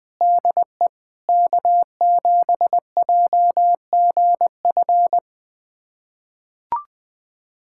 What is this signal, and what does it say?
Morse code, heard as a keyed tone, about 21 wpm, 700 Hz. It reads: DE K7JGF
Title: USA Callsign DE Callsign - 5 Characters - 20wpm